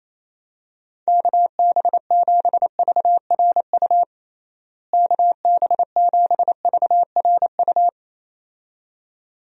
Morse code, heard as a keyed tone, about 27 wpm, 700 Hz. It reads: KB7VRU KB7VRU
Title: USA Call Signs - 6 Characters - Repeated 2x - 28wpm